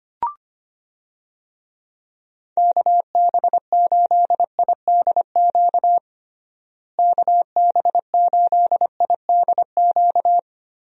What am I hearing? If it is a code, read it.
Morse code, heard as a keyed tone, about 25 wpm, 700 Hz. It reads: KB8IDQ KB8IDQ
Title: USA Call Signs - 6 Characters - Repeated 3x - 25wpm